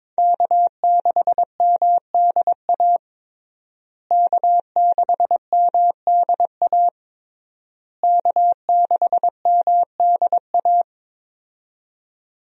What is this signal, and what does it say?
Morse code, heard as a keyed tone, about 22 wpm, 700 Hz. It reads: K6MDA K6MDA K6MDA
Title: USA Call Signs - 5 Characters - Repeated 3x - 22wpm